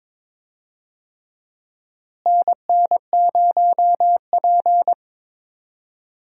Morse code, heard as a keyed tone, about 22 wpm, 700 Hz. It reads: NN0P
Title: USA Call Signs - 4 Characters - Repeated 3x - 22wpm